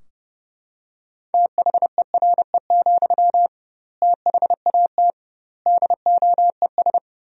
Morse code, heard as a keyed tone, about 30 wpm, 700 Hz. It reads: THERE, THAT DOES
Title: 2x Word-Spacing - Sentences from Top 200 Words - 30wpm